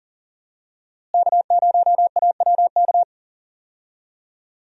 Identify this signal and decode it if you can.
Morse code, heard as a keyed tone, about 40 wpm, 700 Hz. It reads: K0AWK
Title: USA Call Signs - 5 Characters - Repeated 3x - 40wpm